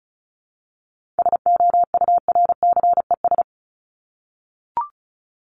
Morse code, heard as a keyed tone, about 34 wpm, 700 Hz. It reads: SOURCES
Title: ES:S Suffix - Top 100 Words with Base Words 35wpm